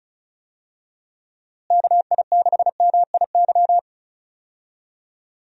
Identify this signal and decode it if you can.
Morse code, heard as a keyed tone, about 36 wpm, 700 Hz. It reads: KI6MIY
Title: USA Call Signs - 6 Characters - Repeated 2x - 35wpm